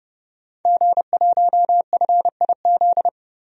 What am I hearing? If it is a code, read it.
Morse code, heard as a keyed tone, about 30 wpm, 700 Hz. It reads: G1FIZ
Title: DXCC-100 Call Signs - 5 Characters - 30wpm